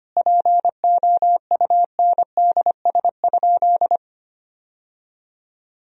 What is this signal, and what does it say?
Morse code, heard as a keyed tone, about 25 wpm, 700 Hz. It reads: POUNDS?
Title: Sentences from Top 1000 Words - Review Part 2 - 25wpm